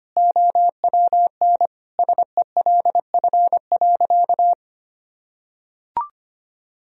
Morse code, heard as a keyed tone, about 25 wpm, 700 Hz. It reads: OWN SELF.
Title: Sentences from Top 1000 Words - Review Part 1 - 25wpm